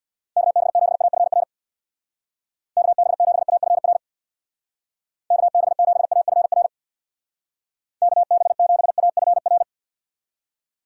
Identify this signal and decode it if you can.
Morse code, heard as a keyed tone, about 61 wpm, 700 Hz. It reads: KB7AFR KB7AFR KB7AFR KB7AFR
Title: Speed Racing - USA 2x3 Call Signs 50wpm